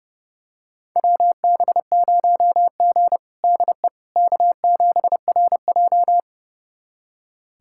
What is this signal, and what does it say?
Morse code, heard as a keyed tone, about 32 wpm, 700 Hz. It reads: WB0G DE K7RJ
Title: USA Callsign DE Callsign - 4 Characters - 30wpm